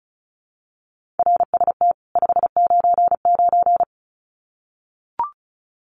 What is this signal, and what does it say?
Morse code, heard as a keyed tone, about 34 wpm, 700 Hz. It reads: RST 599
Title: Common QSO Phrases 35wpm